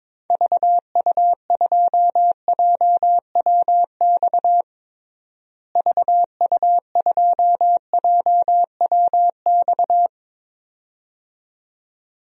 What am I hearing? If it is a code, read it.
Morse code, heard as a keyed tone, about 22 wpm, 700 Hz. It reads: VU2JWX VU2JWX
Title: DXCC-100 Call Signs - 6 Characters - Repeated 2x - 22wpm